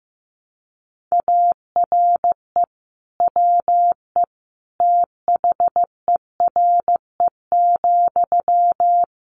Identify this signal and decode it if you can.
Morse code, heard as a keyed tone, about 15 wpm, 700 Hz. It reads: ARE WE THERE,
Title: Sentences from Top 1300 Words 15wpm